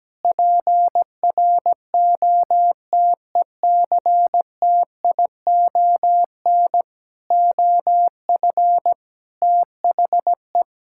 Morse code, heard as a keyed tone, about 17 wpm, 700 Hz. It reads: PROTECTION OF THE
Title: Sentences from Top 1600 Words 17wpm